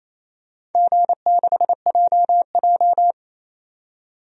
Morse code, heard as a keyed tone, about 28 wpm, 700 Hz. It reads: G6JJ
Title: DXCC-100 Call Signs - 4 Characters - 28wpm